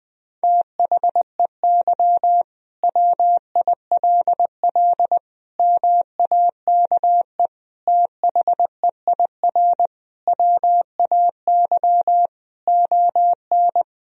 Morse code, heard as a keyed tone, about 20 wpm, 700 Hz. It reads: THEY WILL MAKE THEIR WAY ON
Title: Sentences from Top 1000 Words - Review Part 3 - 20wpm